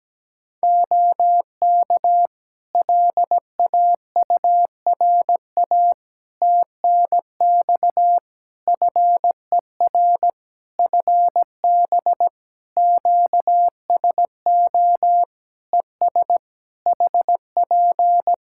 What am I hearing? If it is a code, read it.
Morse code, heard as a keyed tone, about 17 wpm, 700 Hz. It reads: OK LAURA TNX FER FB QSO ES HP
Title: K1USN SST QSO Practice 17wpm